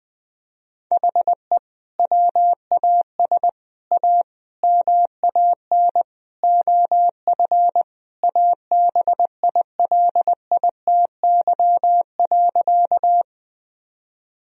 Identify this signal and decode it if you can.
Morse code, heard as a keyed tone, about 20 wpm, 700 Hz. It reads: HE WAS A MAN OF ABILITY.
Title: Sentences from Top 1000 Words - Review Part 3 - 20wpm